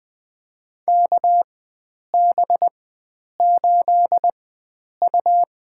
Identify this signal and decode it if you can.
Morse code, heard as a keyed tone, about 20 wpm, spaced as for 9 wpm, 700 Hz. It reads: KB8U
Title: USA Call Signs - 4x Character Spacing - 20wpm